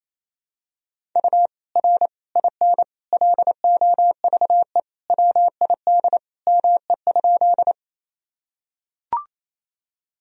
Morse code, heard as a keyed tone, about 28 wpm, 700 Hz. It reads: U R IN LOVE WID ME?
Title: Sentences from Top 400 Words and Abbreviations 28wpm